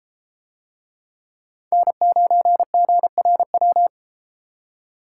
Morse code, heard as a keyed tone, about 30 wpm, 700 Hz. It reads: N9GRW